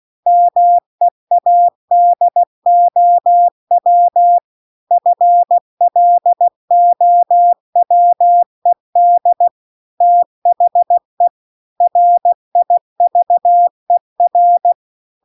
Morse code, heard as a keyed tone, about 16 wpm, 700 Hz. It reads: MEADOW FLOWED THE RIVER